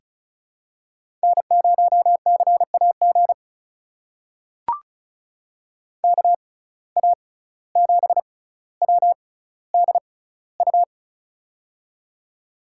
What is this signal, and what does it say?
Morse code, heard as a keyed tone, about 34 wpm, spaced as for 17 wpm, 700 Hz. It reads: N0CAG KA7WDU
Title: USA Call Signs - 6x Character Spacing - 35wpm